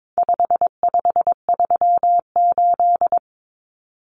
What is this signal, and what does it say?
Morse code, heard as a keyed tone, about 22 wpm, 700 Hz. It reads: 5538
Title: Common Cut Numbers - 4-Digit 22wpm